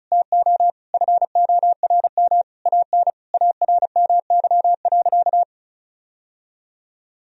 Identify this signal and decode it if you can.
Morse code, heard as a keyed tone, about 36 wpm, 700 Hz. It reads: TO FORM AN ARMY.
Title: Sentences from Top 1000 Words - Review Part 1 - 35wpm